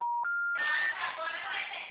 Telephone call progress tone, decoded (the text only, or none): special information tone